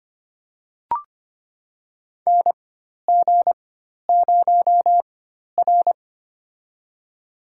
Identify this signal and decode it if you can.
Morse code, heard as a keyed tone, about 25 wpm, spaced as for 12 wpm, 700 Hz. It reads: NG0R
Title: USA Call Signs - 4 Characters - 4x Character Spacing - 25wpm